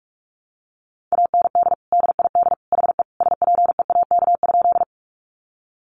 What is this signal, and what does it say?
Morse code, heard as a keyed tone, about 44 wpm, 700 Hz. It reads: AND DID HE SPEAK?
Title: Sentences from Top 1100 Words 45wpm